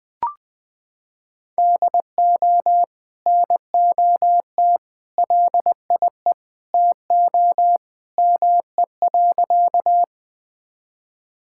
Morse code, heard as a keyed tone, about 20 wpm, 700 Hz. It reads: DO NOT LIE TO ME.
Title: Sentences from Top 1600 Words 20wpm